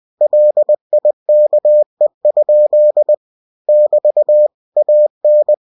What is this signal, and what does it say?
Morse code, heard as a keyed tone, about 20 wpm, 600 Hz. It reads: LIKE? <BT> AN
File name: cw-books / winnie-the-pooh / winnie-20wpm-page-0105